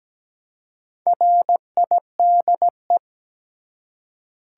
Morse code, heard as a keyed tone, about 17 wpm, 700 Hz. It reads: RIDE
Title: Four Letter Words - 17wpm